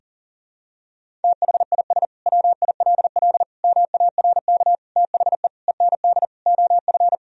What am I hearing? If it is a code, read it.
Morse code, heard as a keyed tone, about 40 wpm, 700 Hz. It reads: THIS WILL MARK THE END OF
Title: Sentences from Top 1000 Words - Review Part 1 - 40wpm